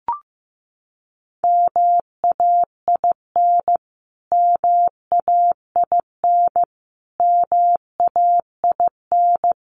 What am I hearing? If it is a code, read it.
Morse code, heard as a keyed tone, about 15 wpm, 700 Hz. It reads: MAIN MAIN MAIN
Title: Top 300 Words Repeated 15wpm